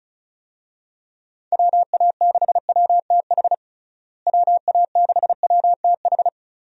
Morse code, heard as a keyed tone, about 34 wpm, 700 Hz. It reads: WA6WTH WA6WTH
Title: USA Call Signs - 6 Characters - Repeated 3x - 35wpm